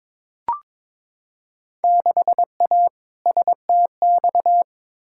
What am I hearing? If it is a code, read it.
Morse code, heard as a keyed tone, about 21 wpm, 700 Hz. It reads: 6A STX
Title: Field Day Exchanges - 22wpm